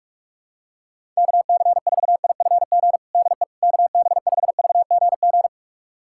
Morse code, heard as a keyed tone, about 45 wpm, 700 Hz. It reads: KC4IFG DE KB5VGG
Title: USA Callsign DE Callsign - 6 Characters - 45wpm